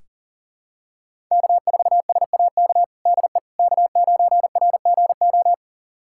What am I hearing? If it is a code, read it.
Morse code, heard as a keyed tone, about 40 wpm, 700 Hz. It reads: K4SAK DE K9RGO
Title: USA Callsign DE Callsign - 5 Characters - 40wpm